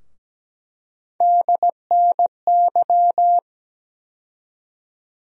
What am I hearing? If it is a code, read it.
Morse code, heard as a keyed tone, about 17 wpm, 700 Hz. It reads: DNY
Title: USA Call Sign Postfixes - 3 Letters 17wpm